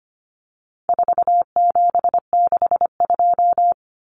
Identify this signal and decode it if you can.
Morse code, heard as a keyed tone, about 25 wpm, 700 Hz. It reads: 4762
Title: Numbers - 4-Digit 25wpm